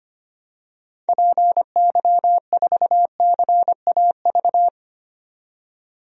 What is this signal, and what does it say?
Morse code, heard as a keyed tone, about 25 wpm, 700 Hz. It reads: PY4CAV
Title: DXCC-100 Call Signs - 6 Characters - 25wpm